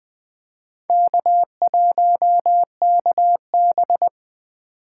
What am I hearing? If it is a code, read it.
Morse code, heard as a keyed tone, about 20 wpm, 700 Hz. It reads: K1KB
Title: USA Call Signs - 4 Characters - 3x Character Spacing - 20wpm